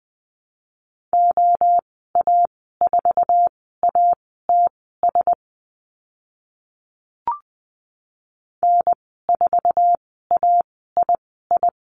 Morse code, heard as a keyed tone, about 20 wpm, spaced as for 14 wpm, 700 Hz. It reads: OA4ATS N4AII
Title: 069 Lesson 12g - Letter d - Call Signs 20wpm